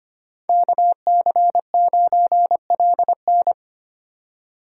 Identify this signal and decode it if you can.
Morse code, heard as a keyed tone, about 25 wpm, 700 Hz. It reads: KC9LN
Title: USA Call Signs - 5 Characters - 4x Character Spacing - 25wpm